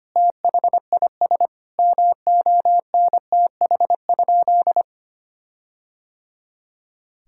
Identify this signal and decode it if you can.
Morse code, heard as a keyed tone, about 25 wpm, 700 Hz. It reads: THIS MONTH?
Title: Sentences from Top 1600 Words 25wpm